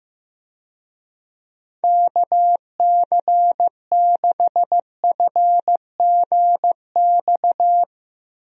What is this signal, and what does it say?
Morse code, heard as a keyed tone, about 15 wpm, 700 Hz. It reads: KC6FGX
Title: USA Call Signs - 6 Characters - Repeated 3x - 15wpm